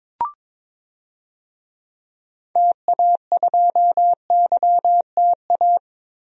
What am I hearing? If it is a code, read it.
Morse code, heard as a keyed tone, about 22 wpm, 700 Hz. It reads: TA2YTA
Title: DXCC-100 Call Signs - 6 Characters - 22wpm